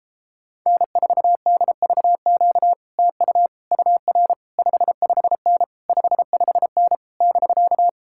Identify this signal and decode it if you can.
Morse code, heard as a keyed tone, about 32 wpm, 700 Hz. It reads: N4DVQ TU UR 55N 55N <BK>